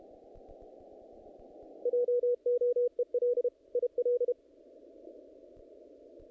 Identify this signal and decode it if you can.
Morse code, heard as a keyed tone, about 31 wpm, 490 Hz. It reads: JOEL IL
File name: contests / naqp-cw-k5tr-op-k5pi- -2000